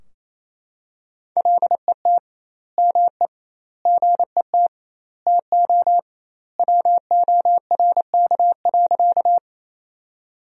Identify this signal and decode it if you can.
Morse code, heard as a keyed tone, about 28 wpm, 700 Hz. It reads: LET ME GET TO WORK.
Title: 2x Word-Spacing - Sentences from Top 300 Words - 28wpm